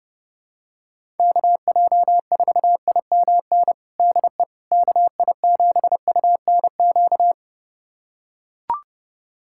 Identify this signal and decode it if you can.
Morse code, heard as a keyed tone, about 29 wpm, 700 Hz. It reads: KJ4IMN DE KI7UNQ
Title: USA Callsign DE Callsign - 6 Characters - 30wpm